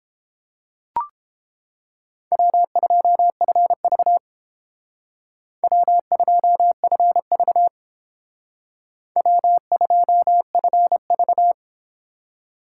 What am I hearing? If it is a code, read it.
Morse code, heard as a keyed tone, about 31 wpm, 700 Hz. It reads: W2FV W2FV W2FV